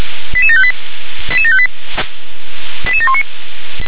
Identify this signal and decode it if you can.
Five-tone selective call (ZVEI), heard as quick runs of five tones, tones of about 70 ms. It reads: 88758 88758 88719